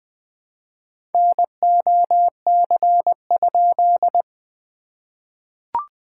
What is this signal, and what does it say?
Morse code, heard as a keyed tone, about 20 wpm, 700 Hz. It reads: NOC?
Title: SOTA QSO Practice 20wpm